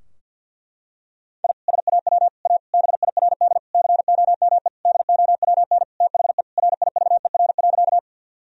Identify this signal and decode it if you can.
Morse code, heard as a keyed tone, about 48 wpm, 700 Hz. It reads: I SAW A BIRD COME DOWN THE RIVER.
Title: Sentences from Top 1000 Words - Review Part 2 - 50wpm